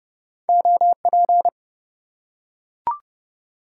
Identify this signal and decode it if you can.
Morse code, heard as a keyed tone, about 30 wpm, 700 Hz. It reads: OP
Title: Top 90 QSO Elements 30wpm